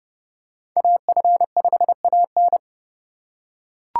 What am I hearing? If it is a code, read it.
Morse code, heard as a keyed tone, about 32 wpm, 700 Hz. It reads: AF5AN
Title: USA Call Signs - 3x Character Spacing - 30wpm